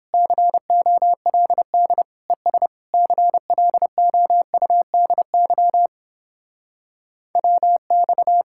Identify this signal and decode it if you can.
Morse code, heard as a keyed tone, about 29 wpm, 700 Hz. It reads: COLD ES CLOUDY WX